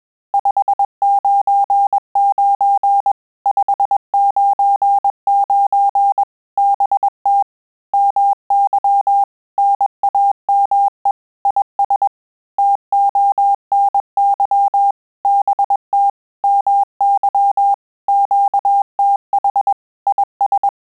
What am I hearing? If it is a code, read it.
Morse code, heard as a keyed tone, about 21 wpm, 800 Hz. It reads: 599 599 BT MY NAME IS TONY BT MY QTH IS
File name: QSO CW 100 CPM